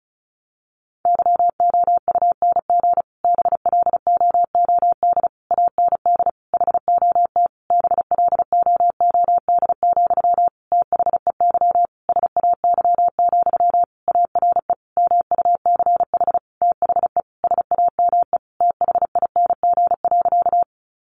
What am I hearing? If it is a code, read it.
Morse code, heard as a keyed tone, about 34 wpm, 700 Hz. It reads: YOUNG BLOOD AND HOT BLOOD, THEY SAY, ARE MUCH THE SAME THING.
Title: Sentences from Top 1100 Words 35wpm